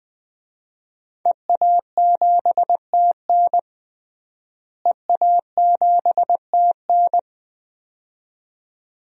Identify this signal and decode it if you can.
Morse code, heard as a keyed tone, about 20 wpm, 700 Hz. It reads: EA7TN EA7TN
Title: DXCC-100 Call Signs - 5 Characters - Repeated 2x - 20wpm